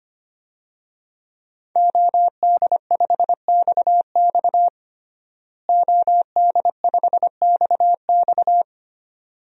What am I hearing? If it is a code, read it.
Morse code, heard as a keyed tone, about 25 wpm, 700 Hz. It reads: OD5XX OD5XX
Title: DXCC-100 Call Signs - 5 Characters - Repeated 2x - 25wpm